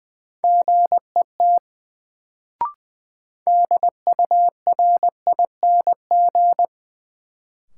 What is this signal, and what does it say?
Morse code, heard as a keyed tone, about 21 wpm, 700 Hz. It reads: GET DURING
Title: Sets of 2 Words - New Word Drill - 20wpm